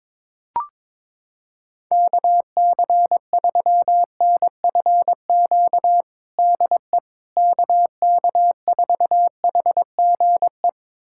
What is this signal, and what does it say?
Morse code, heard as a keyed tone, about 22 wpm, 700 Hz. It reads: KC3NFQ DE KK4HGE
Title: USA Callsign DE Callsign - 6 Characters - 22wpm